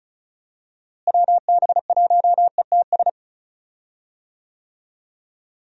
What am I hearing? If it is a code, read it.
Morse code, heard as a keyed tone, about 34 wpm, 700 Hz. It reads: WB1ETS
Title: USA Call Signs - 6 Characters - Repeated 2x - 35wpm